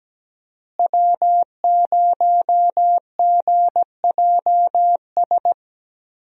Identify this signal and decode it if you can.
Morse code, heard as a keyed tone, about 17 wpm, 700 Hz. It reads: W0GJS
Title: USA Call Signs - 5 Characters - Repeated 3x - 17wpm